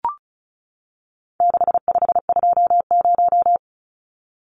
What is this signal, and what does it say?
Morse code, heard as a keyed tone, about 34 wpm, 700 Hz. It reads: 6520
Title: Numbers - 4-Digit 35wpm